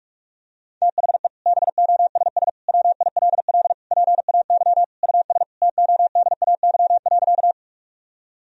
Morse code, heard as a keyed tone, about 42 wpm, 700 Hz. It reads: THE BOSS WILL PAY US TODAY.
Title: Sentences from Top 1000 Words - Review Part 4 - 45wpm